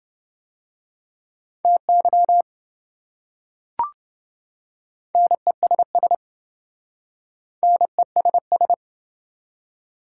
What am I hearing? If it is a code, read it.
Morse code, heard as a keyed tone, about 27 wpm, 700 Hz. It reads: TY NESS NESS